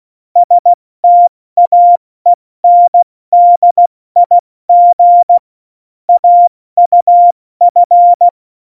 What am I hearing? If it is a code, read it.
Morse code, heard as a keyed tone, about 16 wpm, 700 Hz. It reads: STAENDIG AUF
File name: Morsekurs / Witze / Witz 26. Januar 2021, 70 F80